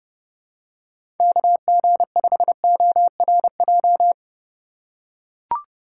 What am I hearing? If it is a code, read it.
Morse code, heard as a keyed tone, about 30 wpm, 690 Hz. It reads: KG5ORJ
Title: USA Call Signs - 3x Character Spacing - 30wpm